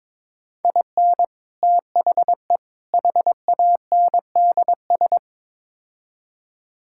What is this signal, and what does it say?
Morse code, heard as a keyed tone, about 22 wpm, 700 Hz. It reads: IN THE HANDS
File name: practice / 22wpm / Build A Phrase - 5-Word Phrases - 22wpm